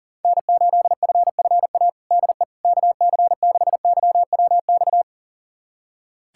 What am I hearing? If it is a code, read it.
Morse code, heard as a keyed tone, about 40 wpm, 700 Hz. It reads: N8FFA DE KC6YWX
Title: USA Callsign DE Callsign 40wpm